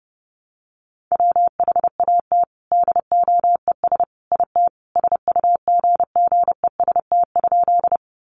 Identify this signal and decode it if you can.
Morse code, heard as a keyed tone, about 30 wpm, 700 Hz. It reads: WHAT DOES IT SUGGEST?
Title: Sentences from Top 1300 Words 30wpm